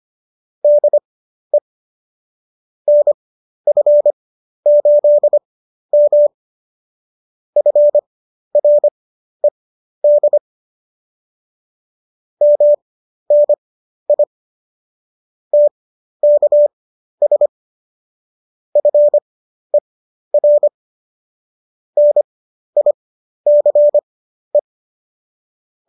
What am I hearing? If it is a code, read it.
Morse code, heard as a keyed tone, about 25 wpm, spaced as for 12 wpm, 590 Hz. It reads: DE NF8M FRED MNI TKS FER NICE